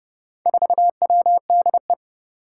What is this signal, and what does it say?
Morse code, heard as a keyed tone, about 30 wpm, 700 Hz. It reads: 4WDE
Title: Four Challenging Letter-Number Combinations 30wpm